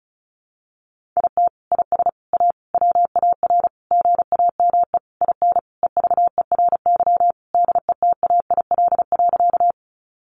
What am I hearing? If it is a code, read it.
Morse code, heard as a keyed tone, about 34 wpm, 700 Hz. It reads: IT IS A WAR GAME IN EVERY DETAIL.
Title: Sentences from Top 1500 Words 35wpm